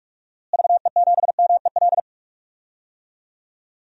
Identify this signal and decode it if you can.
Morse code, heard as a keyed tone, about 42 wpm, 700 Hz. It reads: VE7MEL
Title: DXCC-100 Call Signs - 6 Characters - Repeated 3x - 45wpm